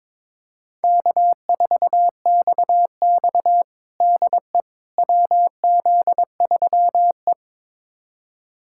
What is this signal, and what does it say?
Morse code, heard as a keyed tone, about 21 wpm, 700 Hz. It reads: K4XX DE WZ3E
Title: USA Callsign DE Callsign - 4 Characters - 22wpm